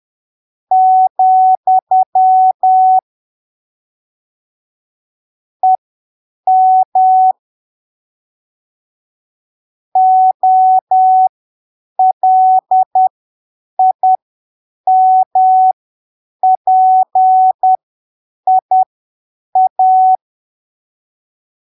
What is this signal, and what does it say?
Morse code, heard as a keyed tone, about 10 wpm, spaced as for 7 wpm, 750 Hz. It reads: , EM OLIMPIA
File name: arquivos / texto / texto 08 10ppm